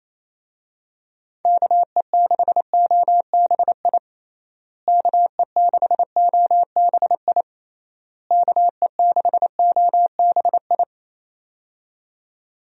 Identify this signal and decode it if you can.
Morse code, heard as a keyed tone, about 28 wpm, 700 Hz. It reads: KE6OBI KE6OBI KE6OBI
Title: USA Call Signs - 6 Characters - Repeated 3x - 28wpm